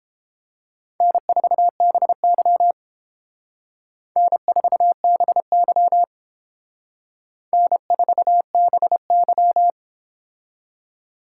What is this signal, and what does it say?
Morse code, heard as a keyed tone, about 30 wpm, 700 Hz. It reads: N4BY N4BY N4BY